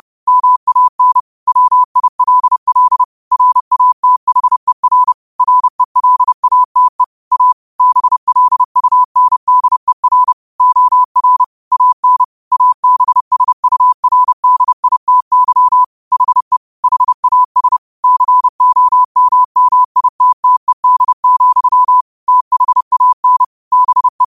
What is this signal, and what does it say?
Morse code, heard as a keyed tone, about 30 wpm, 1000 Hz. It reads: MAN WILL RATHER RELATE A BLUNDER OR AN ABSURDITY HE HAS COMMITTED, THAN BE
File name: qotdfiles / 30wpm